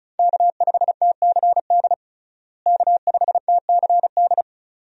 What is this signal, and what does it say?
Morse code, heard as a keyed tone, about 36 wpm, 700 Hz. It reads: K5TCD K5TCD
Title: USA Call Signs - 5 Characters - Repeated 2x - 35wpm